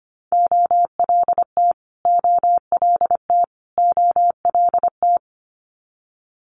Morse code, heard as a keyed tone, about 25 wpm, 700 Hz. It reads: OLT OLT OLT
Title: Multi-letter Phonemes Repeated 3x 25wpm